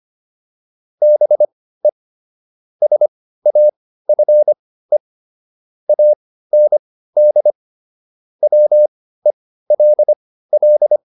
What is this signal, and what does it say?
Morse code, heard as a keyed tone, about 25 wpm, spaced as for 15 wpm, 600 Hz. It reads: BE SAFE AND WELL
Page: intermediate-practice-files